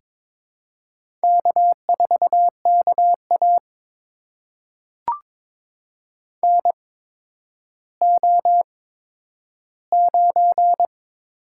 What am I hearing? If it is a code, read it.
Morse code, heard as a keyed tone, about 22 wpm, spaced as for 11 wpm, 700 Hz. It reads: K4KA NO9
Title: USA Call Signs - 4 Characters - 8x Character Spacing - 22wpm